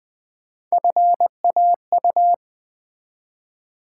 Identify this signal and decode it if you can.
Morse code, heard as a keyed tone, about 20 wpm, 700 Hz. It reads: FAU
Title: Build A Word - 1 Letter At A Time - 5-Letter Words - 20wpm